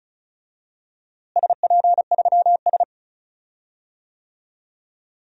Morse code, heard as a keyed tone, about 36 wpm, 700 Hz. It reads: SP3S
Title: DXCC-100 Call Signs - 4 Characters - Repeated 2x - 35wpm